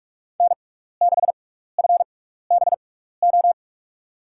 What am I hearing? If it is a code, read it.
Morse code, heard as a keyed tone, about 44 wpm, spaced as for 15 wpm, 700 Hz. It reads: N6FBO
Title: USA Call Signs - 5 Characters - 6x Character Spacing - 45wpm